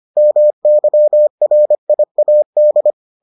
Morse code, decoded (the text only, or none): MYRIAD